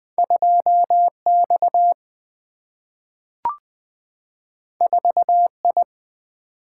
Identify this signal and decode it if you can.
Morse code, heard as a keyed tone, about 20 wpm, 700 Hz. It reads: 2X 4I
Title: Single-Letter Single-Number 20wpm